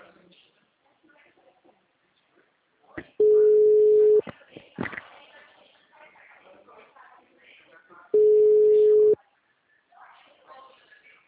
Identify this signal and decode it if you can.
Ringback tone (Europe, 425 Hz), with speech.